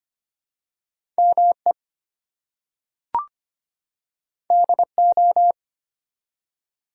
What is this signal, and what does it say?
Morse code, heard as a keyed tone, about 25 wpm, 700 Hz. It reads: ME DO
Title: Two Letter Words - 25wpm